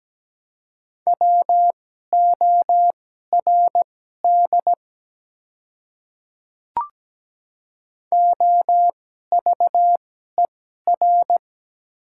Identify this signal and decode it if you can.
Morse code, heard as a keyed tone, about 17 wpm, spaced as for 12 wpm, 690 Hz. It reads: WORD OVER
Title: 2x Character Spacing - Top 100 Words - 17wpm